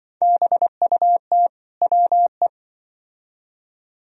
Morse code, heard as a keyed tone, about 24 wpm, 700 Hz. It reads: BUT WE
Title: Speed-Racing - Sets of 2 Words 20wpm